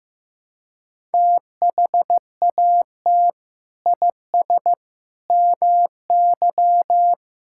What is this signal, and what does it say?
Morse code, heard as a keyed tone, about 15 wpm, 700 Hz. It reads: THAT IS MY
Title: Sentences from Top 1600 Words 15wpm